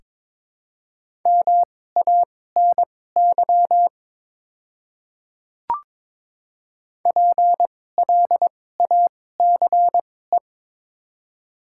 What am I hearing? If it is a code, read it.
Morse code, heard as a keyed tone, about 22 wpm, spaced as for 16 wpm, 700 Hz. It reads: MANY PLACE
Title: 2x Character Spacing - Top 200 Words - 22wpm